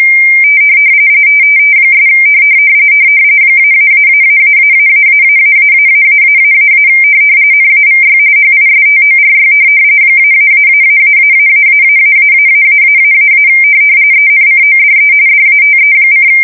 RTTY, about 45.45 baud. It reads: ' -------------------- NAVY HISTORY -------------------- ENCODE DATE: